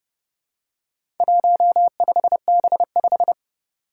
Morse code, heard as a keyed tone, about 30 wpm, 690 Hz. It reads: 15B5
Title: Four Challenging Letter-Number Combinations 30wpm